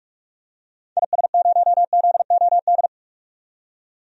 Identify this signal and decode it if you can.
Morse code, heard as a keyed tone, about 42 wpm, 700 Hz. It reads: IS0ZOD
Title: DXCC-100 Call Signs - 6 Characters - 45wpm